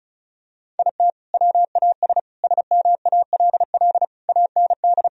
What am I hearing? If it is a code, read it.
Morse code, heard as a keyed tone, about 34 wpm, 700 Hz. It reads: IT WAS SMALL AND
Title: Sentences from Top 1000 Words - Review Part 1 - 35wpm